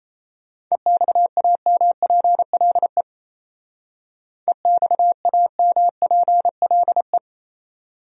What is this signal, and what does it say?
Morse code, heard as a keyed tone, about 34 wpm, 700 Hz. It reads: EXAMPLE EXAMPLE
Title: Speed-Racing Top 300 Words 28wpm